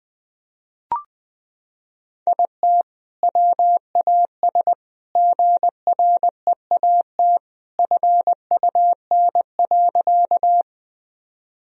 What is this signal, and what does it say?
Morse code, heard as a keyed tone, about 20 wpm, 700 Hz. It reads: IT WAS GREAT FUN.
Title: Sentences from Top 1000 Words - Review Part 4 - 20wpm